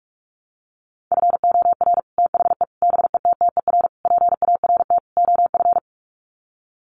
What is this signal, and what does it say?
Morse code, heard as a keyed tone, about 44 wpm, 690 Hz. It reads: FOR THE BETTER PART OF
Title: Sets of 5 Words Encore2 45wpm